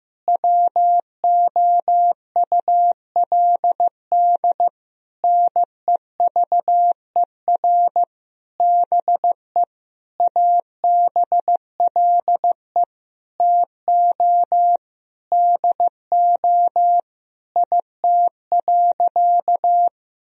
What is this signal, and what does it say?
Morse code, heard as a keyed tone, about 15 wpm, 700 Hz. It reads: WOULD NEVER BE ABLE TO DO IT.
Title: Sentences from Top 1000 Words - Review Part 2 - 15wpm